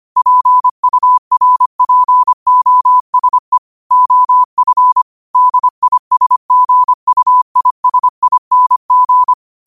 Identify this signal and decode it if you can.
Morse code, heard as a keyed tone, about 25 wpm, 1000 Hz. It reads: PURPOSE OF DISGUISING
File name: qotdfiles / 25wpm